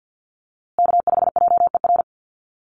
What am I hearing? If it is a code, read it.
Morse code, heard as a keyed tone, about 50 wpm, 700 Hz. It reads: K5JER
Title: USA Call Signs Repeated 3x 50wpm